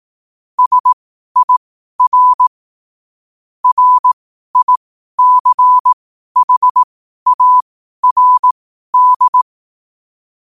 Morse code, heard as a keyed tone, about 18 wpm, spaced as for 12 wpm, 1000 Hz. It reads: SIR RICHARD
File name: qotdfiles / 12wpm